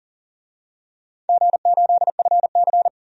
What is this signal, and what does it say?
Morse code, heard as a keyed tone, about 40 wpm, 700 Hz. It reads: G8FC
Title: DXCC-100 Call Signs - 4 Characters - 40wpm